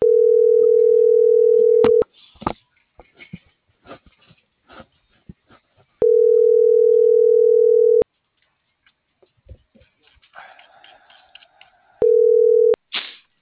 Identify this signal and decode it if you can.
Ringback tone (North America), 440 + 480 Hz.